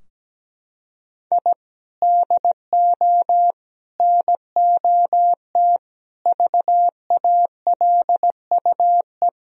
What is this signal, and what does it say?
Morse code, heard as a keyed tone, about 17 wpm, 700 Hz. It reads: I DO NOT VALUE
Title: Sentences from Top 1000 Words - Review Part 4 - 17wpm